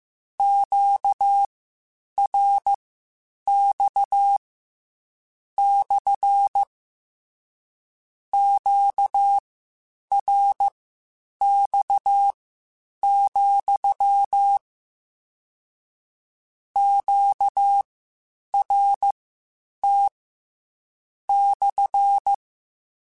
Morse code, heard as a keyed tone, about 15 wpm, spaced as for 8 wpm, 800 Hz. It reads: QRX / QRX, QRT /
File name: Q Calls / Q Calls-mp3s / Q CALLS SPSL @13